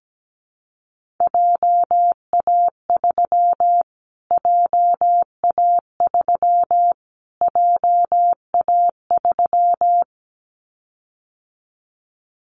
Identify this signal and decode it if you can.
Morse code, heard as a keyed tone, about 17 wpm, 700 Hz. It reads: JA3 JA3 JA3
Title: Callsign Prefixes for Top 25 DXCC - Repeated 3x - 17wpm